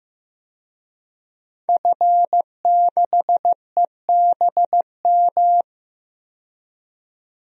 Morse code, heard as a keyed tone, about 16 wpm, 700 Hz. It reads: F6EBM
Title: DXCC-100 Call Signs - 5 Characters - 15wpm